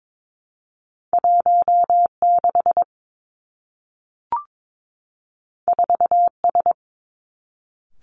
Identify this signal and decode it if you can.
Morse code, heard as a keyed tone, about 22 wpm, 700 Hz. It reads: 1 6 4 S
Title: Two Challenging Letter-Number Combinations 22wpm